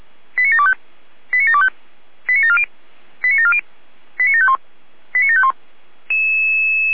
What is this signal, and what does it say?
Five-tone selective call (ZVEI), heard as quick runs of five tones, tones of about 80 ms. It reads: 89725 89725 89740 89740 89751 89751